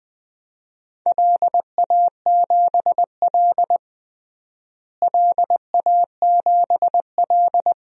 Morse code, heard as a keyed tone, about 20 wpm, 700 Hz. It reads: LA7L LA7L
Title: DXCC-100 Call Signs - 4 Characters - Repeated 3x - 20wpm